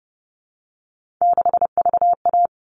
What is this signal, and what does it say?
Morse code, heard as a keyed tone, about 30 wpm, 700 Hz. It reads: 6VA
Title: All Cut Numbers - 3-Digit 30wpm